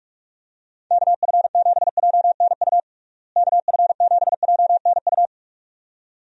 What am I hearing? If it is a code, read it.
Morse code, heard as a keyed tone, about 46 wpm, 700 Hz. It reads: KF7JNU KF7JNU
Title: USA Call Signs - 6 Characters - Repeated 2x - 45wpm